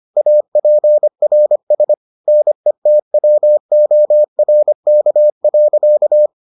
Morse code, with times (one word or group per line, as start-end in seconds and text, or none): 0.2-1.9 APRS
2.3-6.3 NETWORK.